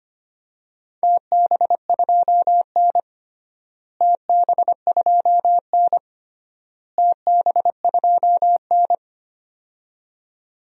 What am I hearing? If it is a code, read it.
Morse code, heard as a keyed tone, about 25 wpm, 700 Hz. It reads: TB2N TB2N TB2N
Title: DXCC-100 Call Signs - 4 Characters - Repeated 3x - 25wpm